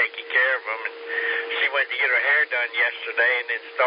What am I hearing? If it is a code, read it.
Emergency Alert System (SAME) tones, heard as harsh buzzing bursts, with speech. It is heard from 0.6 to 3.5 s.